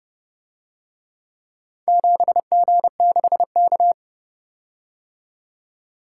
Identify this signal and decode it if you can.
Morse code, heard as a keyed tone, about 29 wpm, 700 Hz. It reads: 7G6K